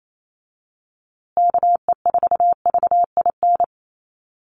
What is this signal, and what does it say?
Morse code, heard as a keyed tone, about 28 wpm, 700 Hz. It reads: KE4VIN
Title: USA Call Signs Repeated 3x 28wpm